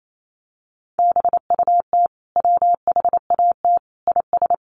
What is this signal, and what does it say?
Morse code, heard as a keyed tone, about 28 wpm, 700 Hz. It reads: BUT WHAT IS
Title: Sentences from Top 100 Words 28wpm